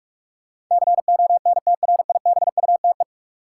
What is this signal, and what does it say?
Morse code, heard as a keyed tone, about 41 wpm, 700 Hz. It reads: CONTRIBUTE
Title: Sentences from Top 1400 Words - New Word Drill Alt - 45wpm